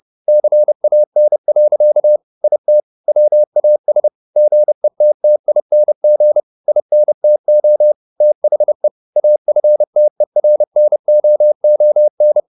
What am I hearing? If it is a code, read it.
Morse code, heard as a keyed tone, about 30 wpm, 600 Hz. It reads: CAN. IT WAS GETTING INTO THE AFTERNOON
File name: the old man and the sea / the old man and the sea 1 30wpm